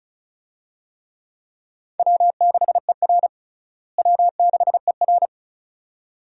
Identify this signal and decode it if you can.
Morse code, heard as a keyed tone, about 36 wpm, 700 Hz. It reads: W6ER W6ER
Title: USA Call Signs - 4 Characters - Repeated 2x - 35wpm